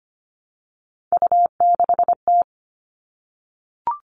Morse code, heard as a keyed tone, about 25 wpm, 700 Hz. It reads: U6T
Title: All Cut Numbers - 3-Digit 25wpm